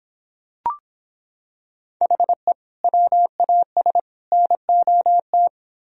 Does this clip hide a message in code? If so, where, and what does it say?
Yes: Morse code from 2.0 to 5.5 s, heard as a keyed tone, about 25 wpm, 700 Hz. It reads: HE WAS NOT